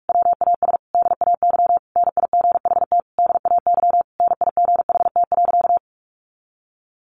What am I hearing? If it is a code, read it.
Morse code, heard as a keyed tone, about 45 wpm, 700 Hz. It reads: WAS DAY NIGHT DAY NIGHT.
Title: A Sound Of Thunder 45wpm